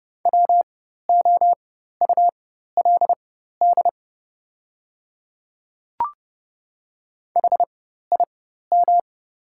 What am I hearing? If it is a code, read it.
Morse code, heard as a keyed tone, about 31 wpm, spaced as for 14 wpm, 700 Hz. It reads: WOULD HIM
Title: 4x Character Spacing - Top 200 Words - 30wpm